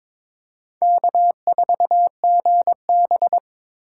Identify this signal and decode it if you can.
Morse code, heard as a keyed tone, about 23 wpm, 700 Hz. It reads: K4GB
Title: USA Call Signs - 4 Characters - 8x Character Spacing - 22wpm